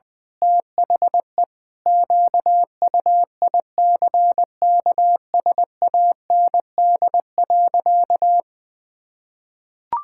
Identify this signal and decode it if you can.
Morse code, heard as a keyed tone, about 20 wpm, 700 Hz. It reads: THE QUICKSAND.
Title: Pangram Challenge v2 - 20wpm